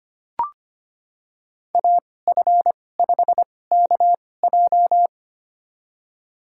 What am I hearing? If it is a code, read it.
Morse code, heard as a keyed tone, about 25 wpm, spaced as for 18 wpm, 700 Hz. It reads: AF5KJ
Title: USA Call Signs - 5 Characters - 2x Character Spacing - 25wpm